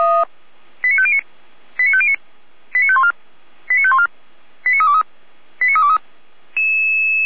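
Five-tone selective call (ZVEI), heard as quick runs of five tones, tones of about 70 ms. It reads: 89509 89509 89514 89514 89323 89323
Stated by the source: ZVEI five-tone selective call, received on an SDR.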